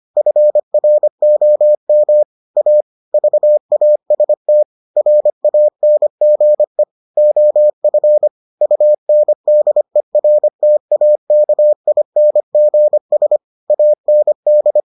Morse code, heard as a keyed tone, about 25 wpm, 600 Hz. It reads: FROM A VAST RANGE OF UNDERTAKINGS AND